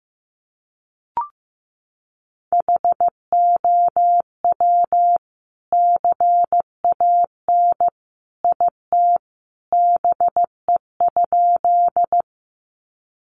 Morse code, heard as a keyed tone, about 15 wpm, 700 Hz. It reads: HOW CAN IT BE?
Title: Sentences from Top 100 Words 15wpm